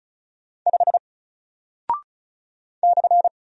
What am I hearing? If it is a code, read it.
Morse code, heard as a keyed tone, about 36 wpm, 700 Hz. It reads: 5 /
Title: Single Character (Letter:Number:Punctuation:Prosign) Alt 35wpm